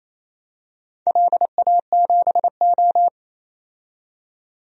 Morse code, heard as a keyed tone, about 28 wpm, 700 Hz. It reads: LA7O
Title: DXCC-100 Call Signs - 4 Characters - 28wpm